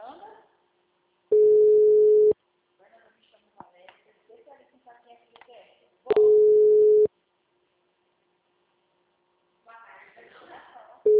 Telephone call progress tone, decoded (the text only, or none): ringback tone